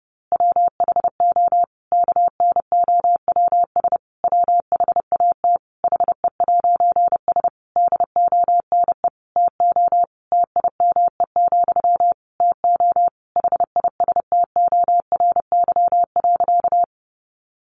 Morse code, heard as a keyed tone, about 30 wpm, 700 Hz. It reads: WHO KNOWS WHAT HE'S DONE TO TIME, TO HISTORY.
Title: A Sound Of Thunder 30wpm